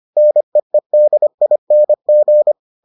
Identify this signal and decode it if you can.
Morse code, heard as a keyed tone, about 25 wpm, 600 Hz. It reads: NEEDING